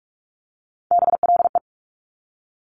Morse code, heard as a keyed tone, about 46 wpm, 700 Hz. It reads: BLE
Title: Top Three Letter Combinations in Words 45wpm